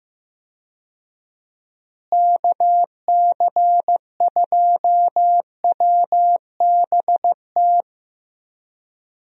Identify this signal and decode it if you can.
Morse code, heard as a keyed tone, about 15 wpm, 700 Hz. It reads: KC2WBT
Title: USA Call Signs - 6 Characters - Repeated 3x - 15wpm